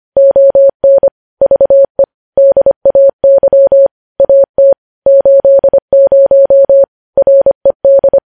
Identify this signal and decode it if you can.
Morse code, heard as a keyed tone, about 25 wpm, 550 Hz. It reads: ON VE DAY AT 80 RED
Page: news-headlines-in-morse-code-at-25-wpm